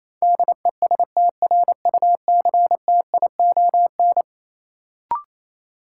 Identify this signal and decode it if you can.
Morse code, heard as a keyed tone, about 28 wpm, 700 Hz. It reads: DESTRUCTION
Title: Sets of 5 Words Encore - New Word Drill - 28wpm